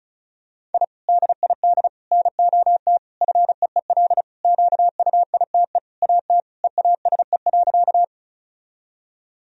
Morse code, heard as a keyed tone, about 34 wpm, 700 Hz. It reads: I DID NOT FEEL QUITE AT EASE.
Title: Sentences from Top 1000 Words - Review Part 1 - 35wpm